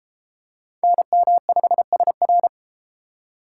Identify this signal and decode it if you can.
Morse code, heard as a keyed tone, about 32 wpm, 700 Hz. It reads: NM5SR